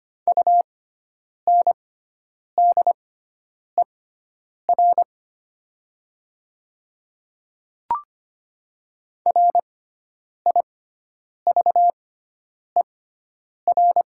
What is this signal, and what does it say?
Morse code, heard as a keyed tone, about 25 wpm, spaced as for 9 wpm, 700 Hz. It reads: UNDER RIVER
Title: 6x Character Spacing - Top 500 Words - 25wpm